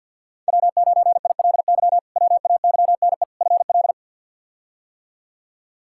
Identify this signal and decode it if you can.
Morse code, heard as a keyed tone, about 51 wpm, 700 Hz. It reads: W9ILY WAYNE FL
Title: K1USN SST QSO Practice - 50wpm